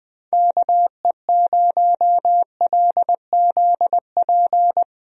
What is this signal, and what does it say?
Morse code, heard as a keyed tone, about 20 wpm, 700 Hz. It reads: KE0LZP
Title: USA Call Signs - 6 Characters - Repeated 3x - 20wpm